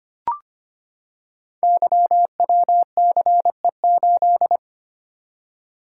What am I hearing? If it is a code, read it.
Morse code, heard as a keyed tone, about 25 wpm, 700 Hz. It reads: YWCE8
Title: Code Groups - 5 Letter:Numbers - 25wpm